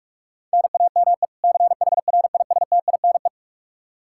Morse code, heard as a keyed tone, about 46 wpm, 700 Hz. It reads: NAME CHRISTINE
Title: Practice QSO - Exchange 1 - Beginning - 45wpm